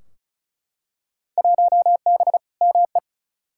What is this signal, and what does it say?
Morse code, heard as a keyed tone, about 34 wpm, 700 Hz. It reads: 1B ME
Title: Field Day Exchanges - 35wpm